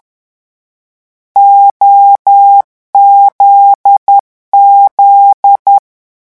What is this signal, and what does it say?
Morse code, heard as a keyed tone, about 10 wpm, 800 Hz. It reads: OZZ
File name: morse mp3 / lecon9-vitesse 10